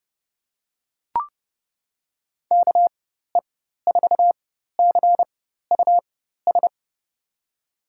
Morse code, heard as a keyed tone, about 30 wpm, spaced as for 14 wpm, 700 Hz. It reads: KE4CUS
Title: USA Call Signs - 6 Characters - 4x Character Spacing - 30wpm